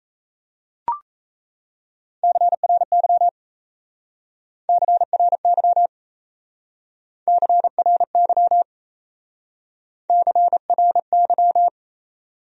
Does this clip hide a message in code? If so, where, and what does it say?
Yes: Morse code from 2.2 to 11.7 s, heard as a keyed tone, about 36 wpm, 700 Hz. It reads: CRY CRY CRY CRY